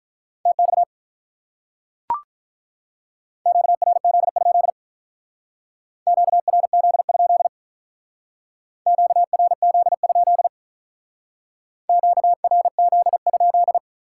Morse code, heard as a keyed tone, about 45 wpm, 700 Hz. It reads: TX QRZ? QRZ? QRZ? QRZ?